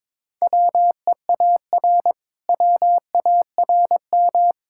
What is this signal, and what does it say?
Morse code, heard as a keyed tone, about 22 wpm, 690 Hz. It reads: WEAR WARM
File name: practice / 22wpm / Easy Sentences with Hints - 22wpm